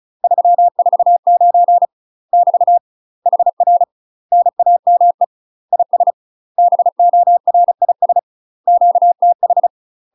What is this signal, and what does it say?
Morse code, heard as a keyed tone, about 34 wpm, 700 Hz. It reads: 349 <BT> HR NAME IS BORIS QTH
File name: UK MASTER QSOs TRAINING / 35WPM / UK MASTER QSOs 0001